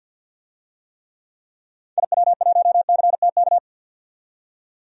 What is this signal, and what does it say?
Morse code, heard as a keyed tone, about 50 wpm, 700 Hz. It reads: IW1CTK
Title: DXCC-100 Call Signs - 6 Characters - 50wpm